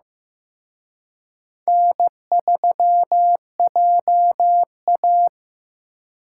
Morse code, heard as a keyed tone, about 18 wpm, 700 Hz. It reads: N3JA
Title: Speed Racing - USA 1x2 Call Signs 15wpm